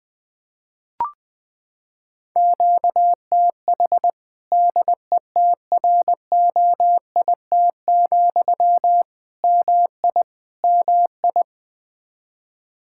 Morse code, heard as a keyed tone, about 20 wpm, 700 Hz. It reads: QTH DETROIT, MI MI
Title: Practice QSO - Exchange 1 - Beginning - 20wpm